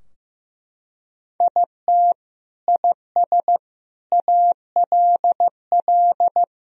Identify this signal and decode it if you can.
Morse code, heard as a keyed tone, about 15 wpm, 700 Hz. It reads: IT IS ALL
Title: Sentences from Top 1000 Words - Review Part 4 - 15wpm